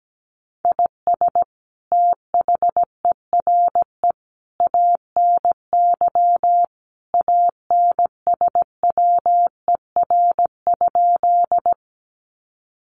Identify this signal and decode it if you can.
Morse code, heard as a keyed tone, about 17 wpm, 700 Hz. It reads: IS THERE ANY ANSWER?
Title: Sentences from Top 300 Words 17wpm